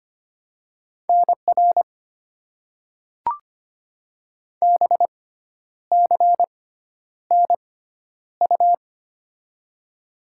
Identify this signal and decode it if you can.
Morse code, heard as a keyed tone, about 26 wpm, spaced as for 10 wpm, 690 Hz. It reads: NR BCNU
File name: Top 90 QSO Elements - 6x Character Spacing - 25wpm